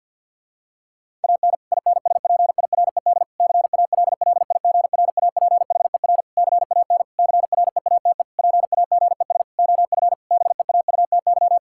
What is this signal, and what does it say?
Morse code, heard as a keyed tone, about 49 wpm, 700 Hz. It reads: AN INSPIRED CALLIGRAPHER CAN CREATE PAGES OF BEAUTY
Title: Pangram Challenge v2 - 50wpm